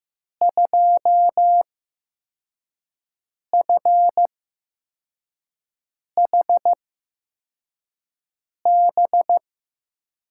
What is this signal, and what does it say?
Morse code, heard as a keyed tone, about 16 wpm, spaced as for 4 wpm, 700 Hz. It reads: 2FHB
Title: USA Call Signs - 8x Character Spacing - 15wpm